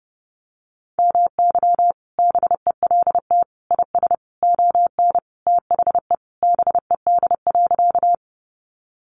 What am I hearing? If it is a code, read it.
Morse code, heard as a keyed tone, about 30 wpm, 700 Hz. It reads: MY BELT IS ON THE BED.
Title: Sentences from Top 1400 Words 30wpm